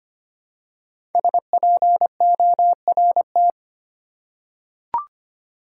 Morse code, heard as a keyed tone, about 25 wpm, 700 Hz. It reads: SPORT
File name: 5 Letter Words - First 3 Letters Given - 25wpm